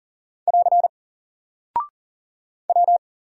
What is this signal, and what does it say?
Morse code, heard as a keyed tone, about 40 wpm, 700 Hz. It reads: <AR> W
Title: Single Character (Letter:Number:Punctuation:Prosign) Alt 40wpm